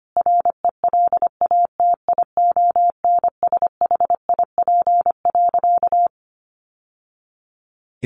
Morse code, heard as a keyed tone, about 25 wpm, 700 Hz. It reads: RELATIONSHIP.
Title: Sentences from Top 1100 Words 25wpm